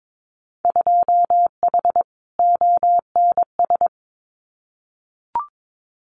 Morse code, heard as a keyed tone, about 22 wpm, 700 Hz. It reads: 2H ONS
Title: Winter Field Day Exchanges Encore 22wpm